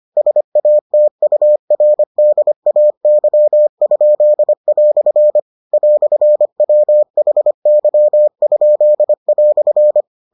Morse code, heard as a keyed tone, about 25 wpm, 600 Hz. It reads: SATURDAY?" "WHY?"